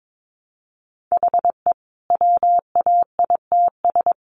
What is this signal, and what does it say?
Morse code, heard as a keyed tone, about 22 wpm, 700 Hz. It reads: HE WAITS
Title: Additional Easy Phrases 22wpm